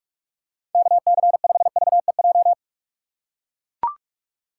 Morse code, heard as a keyed tone, about 41 wpm, 700 Hz. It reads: KC5VEJ
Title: USA Call Signs - 6 Characters - 3x Character Spacing - 45wpm